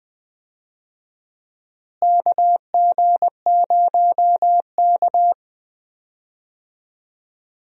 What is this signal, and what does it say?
Morse code, heard as a keyed tone, about 20 wpm, 700 Hz. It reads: KG0K
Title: USA Call Signs - 4 Characters - Repeated 2x - 20wpm